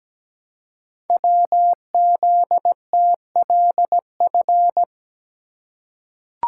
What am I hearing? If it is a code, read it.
Morse code, heard as a keyed tone, about 17 wpm, 700 Hz. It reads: WZTLF
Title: Code Groups - 5 Letters - 17wpm